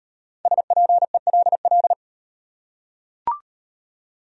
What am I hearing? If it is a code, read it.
Morse code, heard as a keyed tone, about 38 wpm, 700 Hz. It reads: SPELL